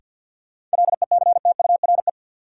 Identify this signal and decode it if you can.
Morse code, heard as a keyed tone, about 52 wpm, 700 Hz. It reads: LECTURE
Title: Sentences from Top 1600 Words - New Word Drill - 50wpm